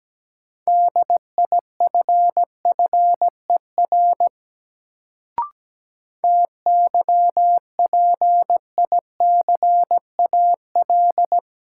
Morse code, heard as a keyed tone, about 17 wpm, 700 Hz. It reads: DIFFER TYPICAL
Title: Sentences from Top 1100 Words - New Word Drill Alt - 17wpm